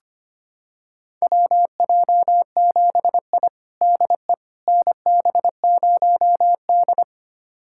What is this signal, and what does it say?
Morse code, heard as a keyed tone, about 25 wpm, 700 Hz. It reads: WJ7I DE NB0D
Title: USA Callsign DE Callsign - 4 Characters - 25wpm